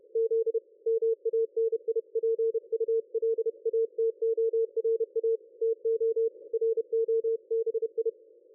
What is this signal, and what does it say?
Morse code, heard as a keyed tone, about 31 wpm, 450 Hz. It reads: Z MANIPULATORA TO ROBI
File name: dlazaawansowanych / nagraniazpasma / ExpertSDR2 line out 23.01.2020 14.38.27